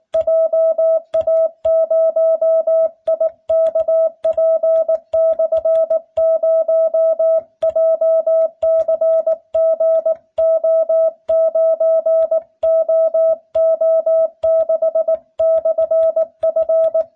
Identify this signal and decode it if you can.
Morse code, heard as a keyed tone, about 19 wpm, 650 Hz. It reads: JA0IKP/0 JCGO9OO6/F